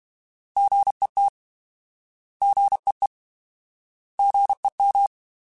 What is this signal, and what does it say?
Morse code, heard as a keyed tone, about 32 wpm, 800 Hz. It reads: GET GEE GEM